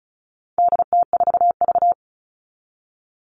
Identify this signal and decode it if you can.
Morse code, heard as a keyed tone, about 34 wpm, 700 Hz. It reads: DT4V
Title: All Cut Numbers - 4-Digit 35wpm